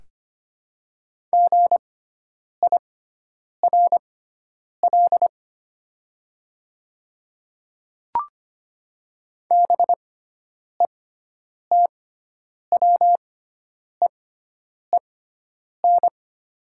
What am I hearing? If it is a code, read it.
Morse code, heard as a keyed tone, about 25 wpm, spaced as for 9 wpm, 700 Hz. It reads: GIRL BETWEEN
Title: 6x Character Spacing - Top 500 Words - 25wpm